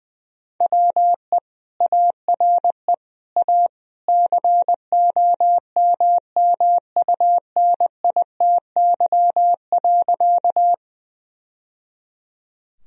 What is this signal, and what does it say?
Morse code, heard as a keyed tone, about 21 wpm, 700 Hz. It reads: WE ARE A COMMUNITY.
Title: Sentences from Top 1000 Words - Review Part 4 - 20wpm